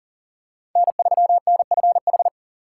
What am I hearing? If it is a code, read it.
Morse code, heard as a keyed tone, about 40 wpm, 700 Hz. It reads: N3NFH
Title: USA Call Signs - 5 Characters - 2x Character Spacing - 40wpm